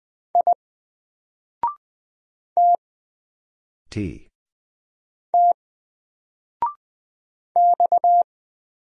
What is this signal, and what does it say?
Morse code, heard as a keyed tone, about 20 wpm, spaced as for 3 wpm, 700 Hz. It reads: I TT X
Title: Single Letter-Number 20wpm